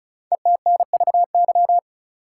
Morse code, heard as a keyed tone, about 34 wpm, 700 Hz. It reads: ETNVY
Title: Code Groups - 5 Letters - 35wpm